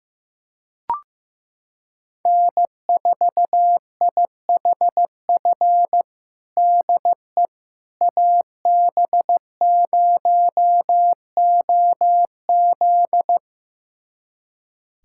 Morse code, heard as a keyed tone, about 15 wpm, 700 Hz. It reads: N4IHF DE AB0OZ
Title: USA Callsign DE Callsign - 5 Characters - 15wpm